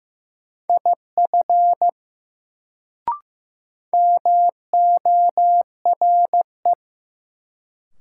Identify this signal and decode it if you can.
Morse code, heard as a keyed tone, about 15 wpm, 700 Hz. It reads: IF MORE
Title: Sentences from Top 100 Words - New Word Drill - 15wpm